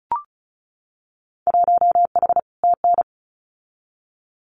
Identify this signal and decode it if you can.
Morse code, heard as a keyed tone, about 34 wpm, 700 Hz. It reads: 1H TN
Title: Winter Field Day Exchanges 35wpm